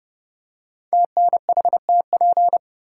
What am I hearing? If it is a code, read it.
Morse code, heard as a keyed tone, about 30 wpm, 700 Hz. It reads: TNHTP
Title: Code Groups - 5 Letters - 30wpm